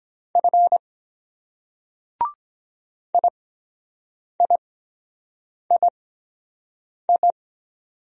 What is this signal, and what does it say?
Morse code, heard as a keyed tone, about 23 wpm, 700 Hz. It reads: F I I I I